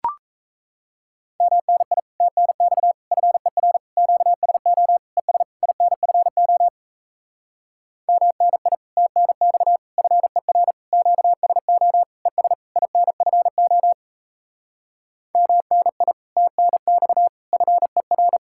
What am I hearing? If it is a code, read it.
Morse code, heard as a keyed tone, about 37 wpm, 700 Hz. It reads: MNI TNX FER QSO ES INFO MNI TNX FER QSO ES INFO MNI TNX FER